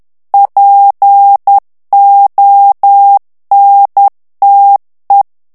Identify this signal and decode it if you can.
Morse code, heard as a keyed tone, about 10 wpm, 800 Hz. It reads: PONTE
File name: morse mp3 / ITALIEN texte clair 1 vitesse 10 mots minute